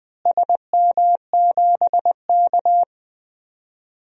Morse code, heard as a keyed tone, about 20 wpm, 700 Hz. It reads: SM7K
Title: DXCC-100 Call Signs - 4 Characters - Repeated 2x - 20wpm